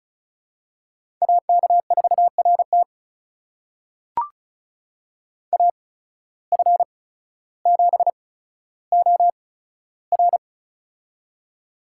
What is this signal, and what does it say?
Morse code, heard as a keyed tone, about 36 wpm, spaced as for 15 wpm, 700 Hz. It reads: AK4RT AF7OR
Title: USA Call Signs - 8x Character Spacing - 35wpm